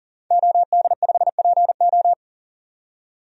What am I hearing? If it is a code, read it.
Morse code, heard as a keyed tone, about 40 wpm, 700 Hz. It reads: OD5PO
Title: DXCC-100 Call Signs - 5 Characters - 40wpm